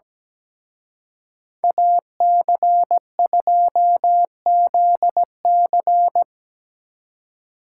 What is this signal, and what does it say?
Morse code, heard as a keyed tone, about 20 wpm, 700 Hz. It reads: AC2ZC
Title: Speed Racing - USA Call Signs 17wpm